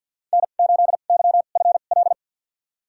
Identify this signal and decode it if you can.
Morse code, heard as a keyed tone, about 48 wpm, spaced as for 35 wpm, 700 Hz. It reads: N8YFL
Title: USA Call Signs - 5 Characters - 2x Character Spacing - 50wpm